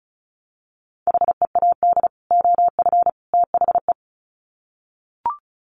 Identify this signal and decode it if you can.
Morse code, heard as a keyed tone, about 34 wpm, 700 Hz. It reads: HEAD OF THE
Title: Sets of 3 Words 35wpm